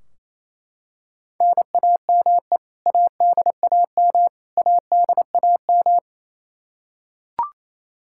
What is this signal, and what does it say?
Morse code, heard as a keyed tone, about 28 wpm, 700 Hz. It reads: NAME ADAM ADAM
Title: K1USN SST QSO Practice 28wpm